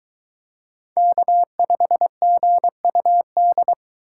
Morse code, heard as a keyed tone, about 22 wpm, 700 Hz. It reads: K5GUD